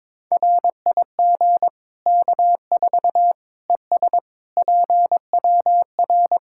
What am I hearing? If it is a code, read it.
Morse code, heard as a keyed tone, about 22 wpm, 700 Hz. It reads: RIG K4 ES PWR
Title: Practice QSO - Exchange 2 - Info - 22wpm